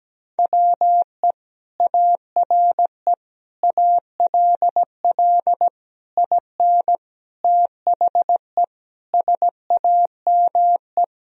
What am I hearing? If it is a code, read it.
Morse code, heard as a keyed tone, about 17 wpm, 700 Hz. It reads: WE ARE ALL IN THE SAME
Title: Sentences from Top 1000 Words - Review Part 3 - 17wpm